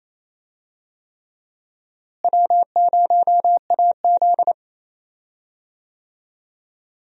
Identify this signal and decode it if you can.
Morse code, heard as a keyed tone, about 28 wpm, 700 Hz. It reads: W0AZ
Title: USA Call Signs - 4 Characters - Repeated 3x - 28wpm